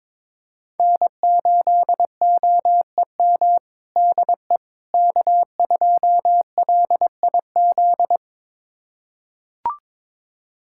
Morse code, heard as a keyed tone, about 22 wpm, 700 Hz. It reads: N8OEM DE K2LIZ
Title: USA Callsign DE Callsign 22wpm